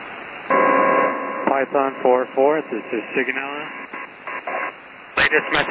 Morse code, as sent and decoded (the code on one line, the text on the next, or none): none